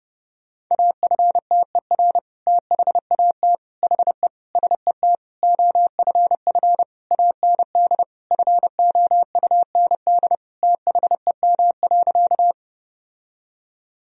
Morse code, heard as a keyed tone, about 32 wpm, 700 Hz. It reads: AFTER THAT HE SET OFF AND FOUND THEM.
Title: Sentences from Top 1000 Words - Review Part 1 - 30wpm